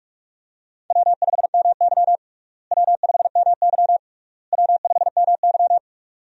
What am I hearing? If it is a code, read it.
Morse code, heard as a keyed tone, about 46 wpm, 700 Hz. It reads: W5MY W5MY W5MY
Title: USA Call Signs - 4 Characters - Repeated 3x - 45wpm